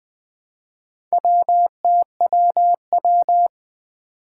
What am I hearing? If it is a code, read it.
Morse code, heard as a keyed tone, about 20 wpm, 700 Hz. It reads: WTWW
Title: Four Challenging Letter-Number Combinations 20wpm